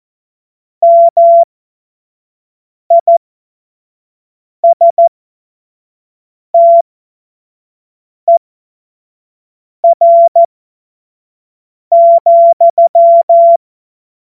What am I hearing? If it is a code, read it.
Morse code, heard as a keyed tone, about 14 wpm, spaced as for 5 wpm, 680 Hz. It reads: MISTER,
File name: Morsekurs / Witze / Witz 01. März 2023 25 F70